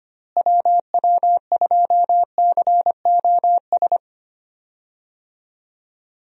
Morse code, heard as a keyed tone, about 25 wpm, 700 Hz. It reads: WW2COS
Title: USA Call Signs - 6 Characters - Repeated 3x - 25wpm